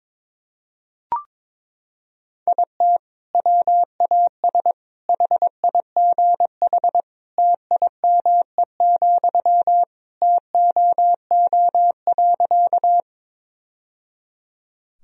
Morse code, heard as a keyed tone, about 22 wpm, 700 Hz. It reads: IT WAS HIGH TIME, TOO.
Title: Sentences from Top 1000 Words - Review Part 1 - 22wpm